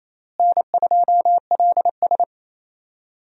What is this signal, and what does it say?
Morse code, heard as a keyed tone, about 27 wpm, 700 Hz. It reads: N2LS
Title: USA Call Signs - 4 Characters - Repeated 2x - 28wpm